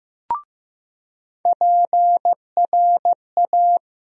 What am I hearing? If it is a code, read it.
Morse code, heard as a keyed tone, about 15 wpm, 700 Hz. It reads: PRA
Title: USA Call Sign Postfixes - 3 Letters 15wpm